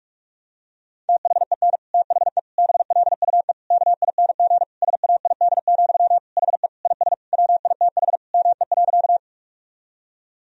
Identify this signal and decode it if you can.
Morse code, heard as a keyed tone, about 46 wpm, 700 Hz. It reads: THEN THE BLUE KING SAID, HE IS WITH ME.
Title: Sentences from Top 1000 Words - Review Part 3 - 45wpm